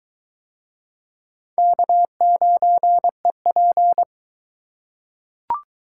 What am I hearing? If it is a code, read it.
Morse code, heard as a keyed tone, about 20 wpm, 700 Hz. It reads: K9EP